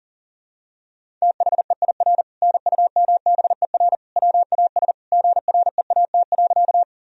Morse code, heard as a keyed tone, about 37 wpm, 700 Hz. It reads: THEIR NUMBER WAS GREAT.
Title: Sentences from Top 1000 Words - Review Part 1 - 40wpm